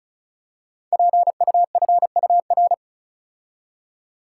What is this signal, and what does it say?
Morse code, heard as a keyed tone, about 36 wpm, 700 Hz. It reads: PUFUR
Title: Code Groups - 5 Letters - 35wpm